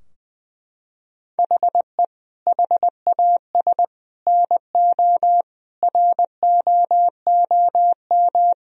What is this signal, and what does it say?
Morse code, heard as a keyed tone, about 20 wpm, 700 Hz. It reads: HE HAS NO ROOM
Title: Sentences from Top 1000 Words - Review Part 4 - 20wpm